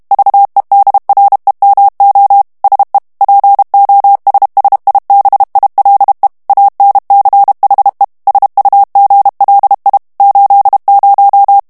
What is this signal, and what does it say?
Morse code, heard as a keyed tone, about 32 wpm, 800 Hz. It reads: VEDREMO SE POSSIBILE ANCHE SUGLI 80
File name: ITALIEN texte clair 1 vitesse 30 mots minute